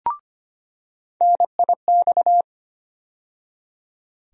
Morse code, heard as a keyed tone, about 25 wpm, 700 Hz. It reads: NIX
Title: USA Call Sign Postfixes - 3 Letters 25wpm